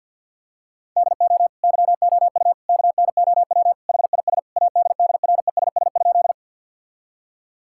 Morse code, heard as a keyed tone, about 50 wpm, 700 Hz. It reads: DO YOU KNOW HIS ADDRESS?
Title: Sentences from Top 1000 Words - Review Part 6 - 50wpm